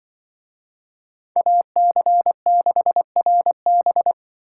Morse code, heard as a keyed tone, about 25 wpm, 700 Hz. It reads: AC6RB
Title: Speed Racing - USA 2x2 Call Signs 20wpm